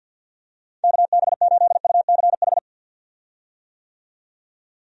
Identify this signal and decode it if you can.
Morse code, heard as a keyed tone, about 49 wpm, 700 Hz. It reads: KB8UCH
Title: USA Call Signs - 6 Characters - 50wpm